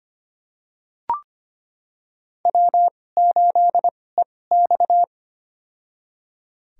Morse code, heard as a keyed tone, about 25 wpm, spaced as for 18 wpm, 700 Hz. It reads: W8EX
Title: USA Call Signs - 4 Characters - 2x Character Spacing - 25wpm